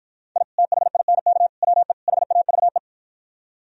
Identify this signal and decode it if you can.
Morse code, heard as a keyed tone, about 52 wpm, 700 Hz. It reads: I THINK WE HAVE